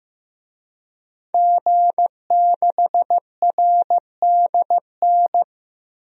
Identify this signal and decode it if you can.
Morse code, heard as a keyed tone, about 15 wpm, 700 Hz. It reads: G6RDN
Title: DXCC-100 Call Signs - 5 Characters - Repeated 2x - 15wpm